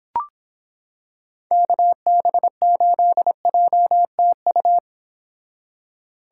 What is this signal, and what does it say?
Morse code, heard as a keyed tone, about 25 wpm, 700 Hz. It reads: KB8JTU